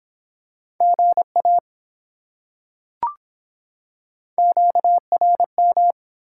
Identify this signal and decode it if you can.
Morse code, heard as a keyed tone, about 24 wpm, 700 Hz. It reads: GA QRM